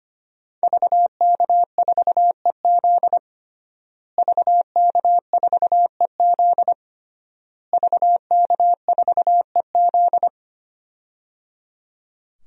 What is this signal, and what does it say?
Morse code, heard as a keyed tone, about 25 wpm, 700 Hz. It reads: VK4EZ VK4EZ VK4EZ
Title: DXCC-100 Call Signs - 5 Characters - Repeated 3x - 25wpm